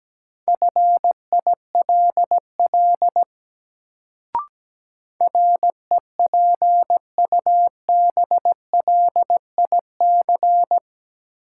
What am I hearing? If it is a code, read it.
Morse code, heard as a keyed tone, about 17 wpm, 700 Hz. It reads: FILL REPUBLIC
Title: Sentences from Top 1500 Words - New Word Drill Alt - 17wpm